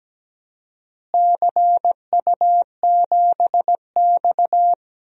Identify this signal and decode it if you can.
Morse code, heard as a keyed tone, about 17 wpm, 700 Hz. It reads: CU7X
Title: DXCC-100 Call Signs - 4 Characters - Repeated 2x - 17wpm